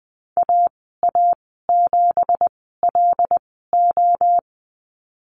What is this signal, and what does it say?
Morse code, heard as a keyed tone, about 20 wpm, spaced as for 15 wpm, 700 Hz. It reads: AA7LO
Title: 173 Lesson 28g - Number 9 - Call Signs 20wpm